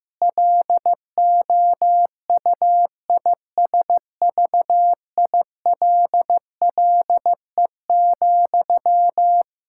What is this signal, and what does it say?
Morse code, heard as a keyed tone, about 15 wpm, 700 Hz. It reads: LOUISVILLE,
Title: K1USN SST QSO Practice 15wpm